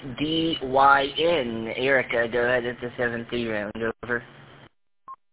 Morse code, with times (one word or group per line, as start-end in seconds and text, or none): none